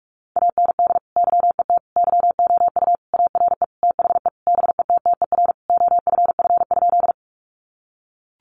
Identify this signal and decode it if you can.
Morse code, heard as a keyed tone, about 44 wpm, 700 Hz. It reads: AND YET YOU ARE THE BETTER OFF?
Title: Sentences from Top 500 Words 45wpm